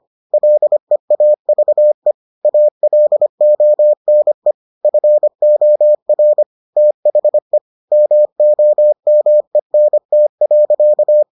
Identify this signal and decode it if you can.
Morse code, heard as a keyed tone, about 25 wpm, 600 Hz. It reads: LEAVE ALONE FOR THE MOMENT.